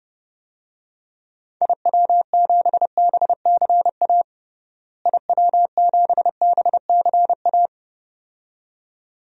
Morse code, heard as a keyed tone, about 32 wpm, 700 Hz. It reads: IW7BCA IW7BCA
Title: DXCC-100 Call Signs - 6 Characters - Repeated 2x - 30wpm